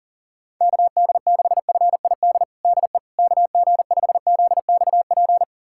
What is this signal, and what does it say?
Morse code, heard as a keyed tone, about 40 wpm, 700 Hz. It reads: KD6FID DE KG5ZXP
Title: USA Callsign DE Callsign - 6 Characters - 40wpm